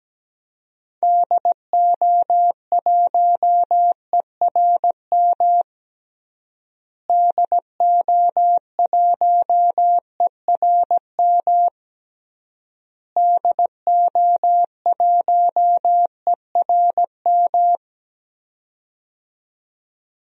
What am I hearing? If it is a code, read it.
Morse code, heard as a keyed tone, about 17 wpm, 700 Hz. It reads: DO1ERM DO1ERM DO1ERM
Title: DXCC-100 Call Signs - 6 Characters - Repeated 3x - 17wpm